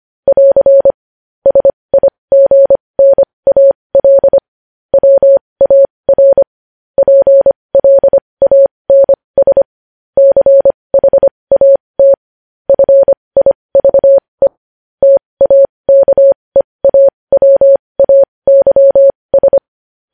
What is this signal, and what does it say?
Morse code, heard as a keyed tone, about 25 wpm, spaced as for 20 wpm, 550 Hz. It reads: <AR> SIGNAL WAR PLANS CHAT FIVE TAKEAWAYS
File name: podcast / News headlines-1x-25x20